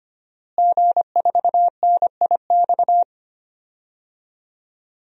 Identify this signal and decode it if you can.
Morse code, heard as a keyed tone, about 25 wpm, 700 Hz. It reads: G4NIX
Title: DXCC-100 Call Signs - 5 Characters - Repeated 3x - 25wpm